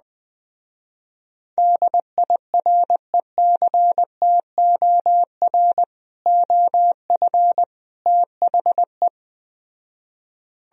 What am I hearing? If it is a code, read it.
Morse code, heard as a keyed tone, about 20 wpm, 700 Hz. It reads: DIRECTOR OF THE